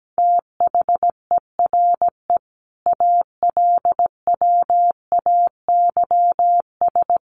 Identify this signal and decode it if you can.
Morse code, heard as a keyed tone, about 17 wpm, 700 Hz. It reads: THERE ALWAYS
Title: Sentences from Top 1100 Words 17wpm